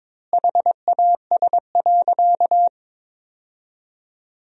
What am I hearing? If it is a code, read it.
Morse code, heard as a keyed tone, about 22 wpm, 700 Hz. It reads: HAS.
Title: 3x Word-Spacing - Sentences from Top 500 Words - 22wpm